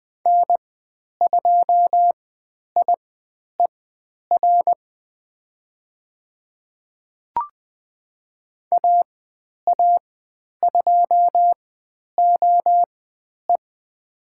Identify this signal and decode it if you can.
Morse code, heard as a keyed tone, about 20 wpm, spaced as for 10 wpm, 700 Hz. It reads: N2IER AA2OE
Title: 102 Lesson 17g - Letter u - Call Signs 20wpm